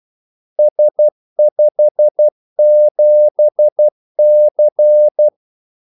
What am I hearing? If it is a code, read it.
Morse code, heard as a keyed tone, about 12 wpm, 600 Hz. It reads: S57C
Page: 2 ekstra ordmellemrum mellem kaldesignalerne